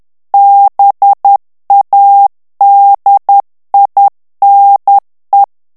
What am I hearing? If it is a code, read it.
Morse code, heard as a keyed tone, about 10 wpm, 800 Hz. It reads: BADINE
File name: CW / FICHIERS-MP3 / FRANCAIS TEXTE CLAIR 2 VITESSE 10 MOTS MINUTES